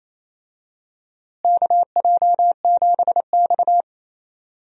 Morse code, heard as a keyed tone, about 28 wpm, 700 Hz. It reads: KJ7X
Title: USA Call Signs - 4 Characters - Repeated 3x - 28wpm